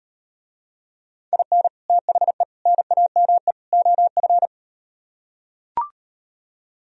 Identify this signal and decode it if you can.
Morse code, heard as a keyed tone, about 38 wpm, 700 Hz. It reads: IN THE NAME OF